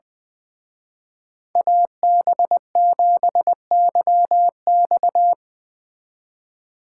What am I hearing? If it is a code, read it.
Morse code, heard as a keyed tone, about 21 wpm, 700 Hz. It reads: AB7YX
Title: Speed Racing - USA 2x2 Call Signs 20wpm